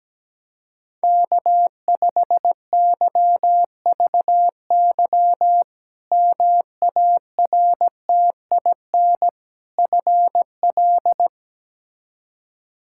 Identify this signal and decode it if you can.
Morse code, heard as a keyed tone, about 17 wpm, 700 Hz. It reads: K5YVY MARTIN FL
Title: K1USN SST QSO Practice - 17wpm